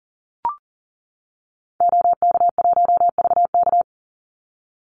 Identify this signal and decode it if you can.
Morse code, heard as a keyed tone, about 40 wpm, 700 Hz. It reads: OK1VK
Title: Call Signs - 40wpm